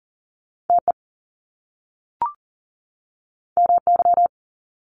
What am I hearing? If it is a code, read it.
Morse code, heard as a keyed tone, about 40 wpm, 700 Hz. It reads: TE MY
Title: Top Two Letter Combinations in Words 40wpm